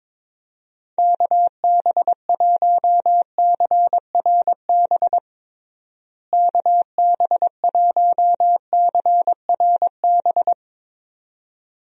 Morse code, heard as a keyed tone, about 22 wpm, 700 Hz. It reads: KB1CRB KB1CRB
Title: USA Call Signs - 6 Characters - Repeated 2x - 22wpm